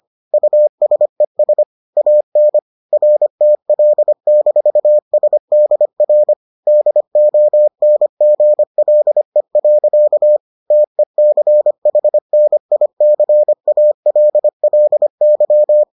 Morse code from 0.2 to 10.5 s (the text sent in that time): USES AN RTL-SDR DONGLE.